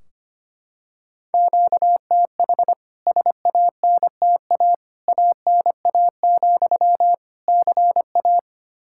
Morse code, heard as a keyed tone, about 25 wpm, 700 Hz. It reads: QTH SANTA ANA, CA
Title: K1USN SST QSO Practice 25wpm